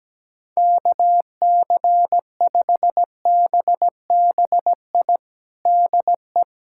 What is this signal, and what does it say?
Morse code, heard as a keyed tone, about 17 wpm, 700 Hz. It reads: KC5BBI DE
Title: USA Callsign DE Callsign - 6 Characters - 17wpm